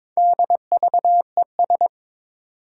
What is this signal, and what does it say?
Morse code, heard as a keyed tone, about 23 wpm, 700 Hz. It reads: DVES
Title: Four Challenging Letter-Number Combinations 22wpm